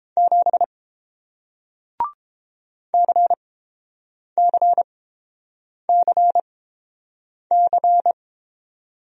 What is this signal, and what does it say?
Morse code, heard as a keyed tone, about 30 wpm, spaced as for 7 wpm, 700 Hz. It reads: 7 CCCC